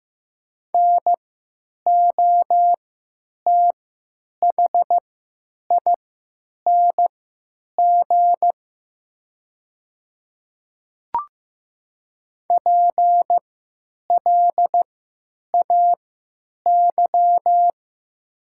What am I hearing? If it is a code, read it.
Morse code, heard as a keyed tone, about 15 wpm, spaced as for 9 wpm, 700 Hz. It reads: NOTHING PLAY
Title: 3x Character Spacing - Top 500 Words - 15wpm